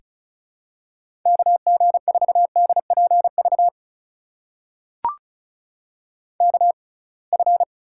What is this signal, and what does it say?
Morse code, heard as a keyed tone, about 34 wpm, spaced as for 26 wpm, 700 Hz. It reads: KG4DPV KF
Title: USA Call Signs - 6 Characters - 6x Character Spacing - 35wpm